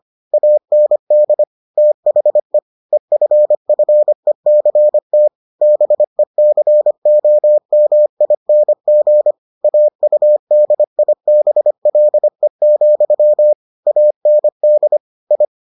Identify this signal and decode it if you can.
Morse code, heard as a keyed tone, about 25 wpm, 600 Hz. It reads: AND THE EFFECT BECOMING AUDIBLE, AND I